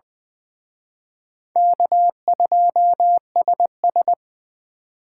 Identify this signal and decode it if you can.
Morse code, heard as a keyed tone, about 24 wpm, 700 Hz. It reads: K2SS
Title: Speed Racing - USA 1x2 Call Signs 20wpm